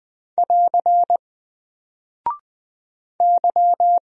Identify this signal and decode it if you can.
Morse code, heard as a keyed tone, about 20 wpm, 700 Hz. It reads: <AR> Y
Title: Single Character (Letter:Number:Punctuation:Prosign) Alt 20wpm